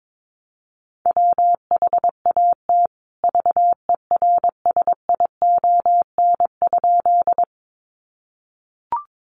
Morse code, heard as a keyed tone, about 22 wpm, 700 Hz. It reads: WHAT VERSION?
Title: Sentences from Top 1400 Words 22wpm